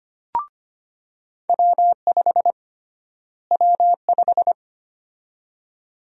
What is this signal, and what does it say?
Morse code, heard as a keyed tone, about 25 wpm, 700 Hz. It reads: W5 W5
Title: USA Call Sign Prefixes - Repeated 2x - 25wpm